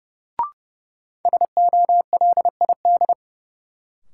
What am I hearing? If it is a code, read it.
Morse code, heard as a keyed tone, about 30 wpm, 700 Hz. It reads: SOLID
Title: Sentences from Top 1500 Words - New Word Drill - 30wpm